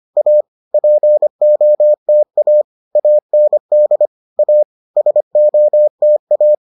Morse code, heard as a keyed tone, about 25 wpm, 600 Hz. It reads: A POTA AND A SOTA